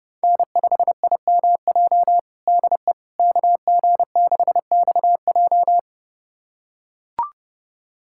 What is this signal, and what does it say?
Morse code, heard as a keyed tone, about 30 wpm, 700 Hz. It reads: N5IMJ DE KG6XJ
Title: USA Callsign DE Callsign - 5 Characters - 30wpm